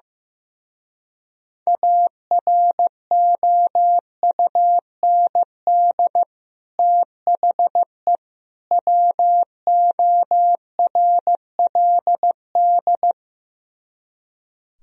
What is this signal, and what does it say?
Morse code, heard as a keyed tone, about 15 wpm, 700 Hz. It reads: AROUND THE WORLD
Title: Speed-Racing - Sets of 3 Words 15wpm